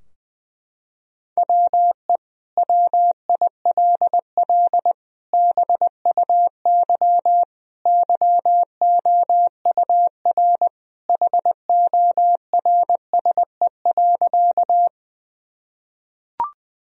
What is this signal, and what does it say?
Morse code, heard as a keyed tone, about 20 wpm, 700 Hz. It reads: WE WILL BUY YOUR HORSE.
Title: Sentences from Top 1000 Words - Review Part 4 - 20wpm